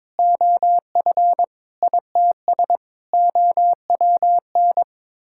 Morse code, heard as a keyed tone, about 22 wpm, 700 Hz. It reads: OF ITS OWN
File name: practice / 22wpm / Sets of 7 Words - 22wpm